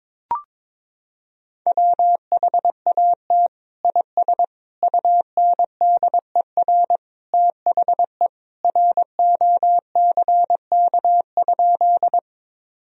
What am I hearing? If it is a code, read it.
Morse code, heard as a keyed tone, about 22 wpm, 700 Hz. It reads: WHAT IS UNDER THE ROCK?
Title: Sentences from Top 1000 Words - Review Part 2 - 22wpm